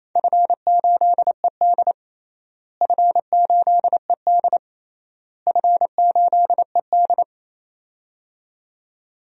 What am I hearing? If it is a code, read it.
Morse code, heard as a keyed tone, about 28 wpm, 700 Hz. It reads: F8ED F8ED F8ED
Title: DXCC-100 Call Signs - 4 Characters - Repeated 3x - 28wpm